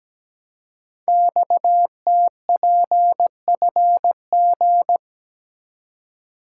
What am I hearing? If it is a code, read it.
Morse code, heard as a keyed tone, about 17 wpm, 700 Hz. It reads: XTPFG
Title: Code Groups - 5 Letters - 17wpm